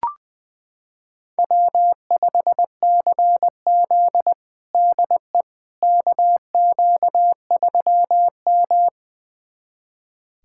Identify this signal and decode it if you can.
Morse code, heard as a keyed tone, about 21 wpm, 700 Hz. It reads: W5CZ DE KQ3M
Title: USA Callsign DE Callsign - 4 Characters - 20wpm